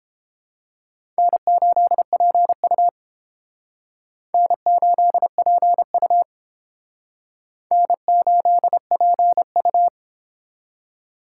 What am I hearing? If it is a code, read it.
Morse code, heard as a keyed tone, about 29 wpm, 700 Hz. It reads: N8PU N8PU N8PU